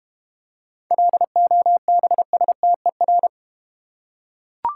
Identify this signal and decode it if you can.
Morse code, heard as a keyed tone, about 30 wpm, 700 Hz. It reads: LOBSTER
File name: Easy Words - Speed-Racing with Hints - 25wpm